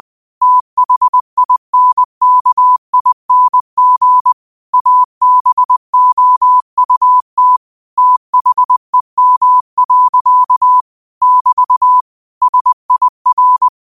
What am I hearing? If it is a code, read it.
Morse code, heard as a keyed tone, about 20 wpm, 1000 Hz. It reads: THINKING ABOUT THEM. <BT> SIR
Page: 20 WPM morse code quotes for Tue, 12 Nov 2024.